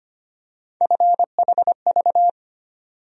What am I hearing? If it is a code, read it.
Morse code, heard as a keyed tone, about 25 wpm, 700 Hz. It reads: FHV
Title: USA Call Sign Postfixes - 3 Letters 25wpm